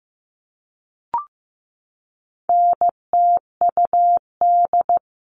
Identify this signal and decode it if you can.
Morse code, heard as a keyed tone, about 15 wpm, 700 Hz. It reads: NTUD
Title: All Cut Numbers - 4-Digit 15wpm